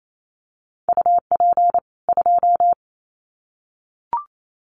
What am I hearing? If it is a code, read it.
Morse code, heard as a keyed tone, about 28 wpm, 700 Hz. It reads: UP 2
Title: Common QSO Phrases 28wpm